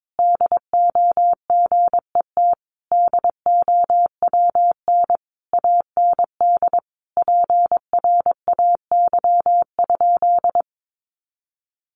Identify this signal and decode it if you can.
Morse code, heard as a keyed tone, about 22 wpm, 700 Hz. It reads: DO GET DOWN AND PRAY?
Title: A Sound Of Thunder 22wpm